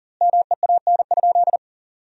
Morse code, heard as a keyed tone, about 40 wpm, 700 Hz. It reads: MEAN?
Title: 2x Word-Spacing - Sentences from Top 200 Words - 40wpm